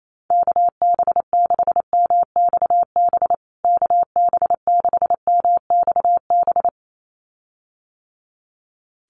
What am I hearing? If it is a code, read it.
Morse code, heard as a keyed tone, about 28 wpm, 700 Hz. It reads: KB6MXB KB6MXB
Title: USA Call Signs Repeated 2x 28wpm